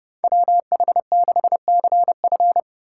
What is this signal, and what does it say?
Morse code, heard as a keyed tone, about 31 wpm, 700 Hz. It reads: WH6CF
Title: USA Call Signs - 5 Characters - 4x Character Spacing - 30wpm